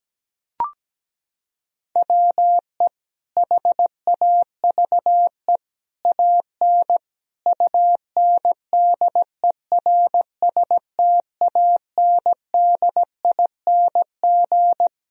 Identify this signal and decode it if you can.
Morse code, heard as a keyed tone, about 17 wpm, 700 Hz. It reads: WE HAVE AN UNDERSTANDING
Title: Sentences from Top 1000 Words - Review Part 3 - 17wpm